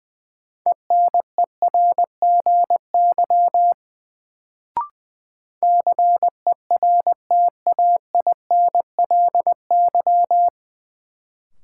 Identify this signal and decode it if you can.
Morse code, heard as a keyed tone, about 20 wpm, 700 Hz. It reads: ENERGY CERTAINLY
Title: Sentences from Top 900 Words - New Word Drill - 20wpm